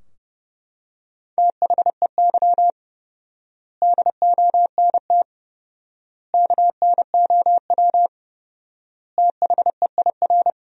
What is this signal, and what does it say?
Morse code, heard as a keyed tone, about 32 wpm, 700 Hz. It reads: THEY DONT KNOW THEIR
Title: 4x Word-Spacing - Sentences from Top 300 Words - 30wpm